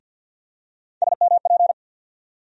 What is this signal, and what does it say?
Morse code, heard as a keyed tone, about 50 wpm, 700 Hz. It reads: SMP
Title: USA Call Sign Postfixes - 3 Letters 50wpm